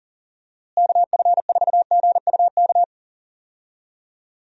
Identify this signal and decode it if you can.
Morse code, heard as a keyed tone, about 40 wpm, 700 Hz. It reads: KF4GUK
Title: USA Call Signs - 6 Characters - Repeated 2x - 40wpm